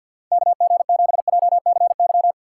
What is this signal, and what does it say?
Morse code, heard as a keyed tone, about 64 wpm, 700 Hz. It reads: KG7JCY